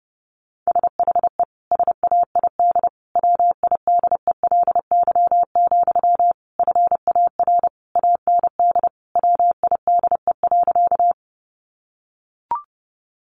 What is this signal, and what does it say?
Morse code, heard as a keyed tone, about 30 wpm, 700 Hz. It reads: SHE SAID WIDELY, FAR AND WIDE.
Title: Sentences from Top 1200 Words 30wpm